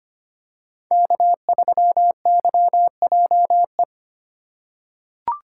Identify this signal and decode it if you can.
Morse code, heard as a keyed tone, about 25 wpm, 700 Hz. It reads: K3YJE
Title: USA Call Signs - 5 Characters - 3x Character Spacing - 25wpm